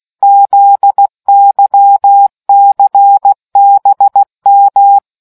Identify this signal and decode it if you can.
Morse code, heard as a keyed tone, about 16 wpm, 800 Hz. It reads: ZYCBM